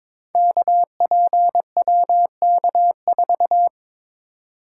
Morse code, heard as a keyed tone, about 22 wpm, 700 Hz. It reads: KPWK4
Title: Code Groups - 5 Letter:Numbers - 22wpm